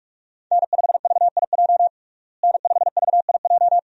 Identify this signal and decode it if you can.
Morse code, heard as a keyed tone, about 43 wpm, 700 Hz. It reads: N5VIJ N5VIJ
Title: USA Call Signs - 5 Characters - Repeated 2x - 45wpm